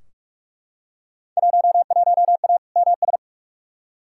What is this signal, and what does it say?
Morse code, heard as a keyed tone, about 43 wpm, 700 Hz. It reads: 11A MS
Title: Field Day Exchanges Encore - 45wpm